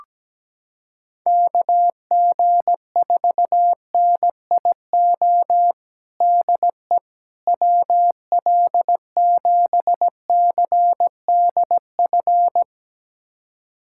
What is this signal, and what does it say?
Morse code, heard as a keyed tone, about 17 wpm, 700 Hz. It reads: KG4NIO DE WL7CDF
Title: USA Callsign DE Callsign - 6 Characters - 17wpm